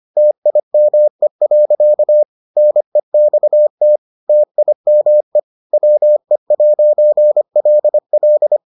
Morse code, heard as a keyed tone, about 25 wpm, 600 Hz. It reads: TIME. NEXT TIME WE'LL